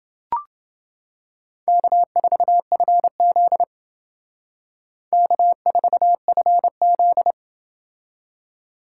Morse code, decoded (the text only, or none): K4FZ K4FZ